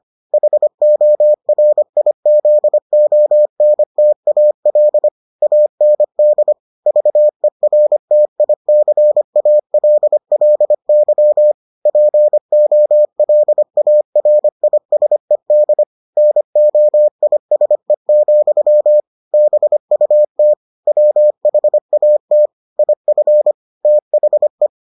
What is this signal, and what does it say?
Morse code, heard as a keyed tone, about 25 wpm, 600 Hz. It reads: HORIZONTAL AND VERTICALLY POLARISED NOISE, BUT WHAT IF THE